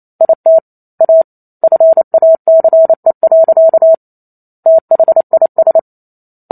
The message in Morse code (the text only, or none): IT A FACE. THIS